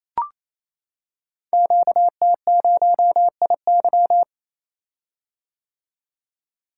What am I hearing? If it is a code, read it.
Morse code, heard as a keyed tone, about 28 wpm, 700 Hz. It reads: QT0IY
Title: Code Groups - 5 Letter:Numbers - 28wpm